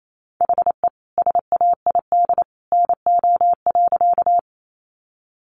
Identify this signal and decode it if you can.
Morse code, heard as a keyed tone, about 28 wpm, 700 Hz. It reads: HE SAID NO.
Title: Sentences from Top 100 Words 28wpm